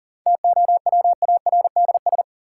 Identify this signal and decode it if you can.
Morse code, heard as a keyed tone, about 40 wpm, 700 Hz. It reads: TOWARDS
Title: Sentences from Top 1000 Words - Review Part 2 - New Word Drill - 40wpm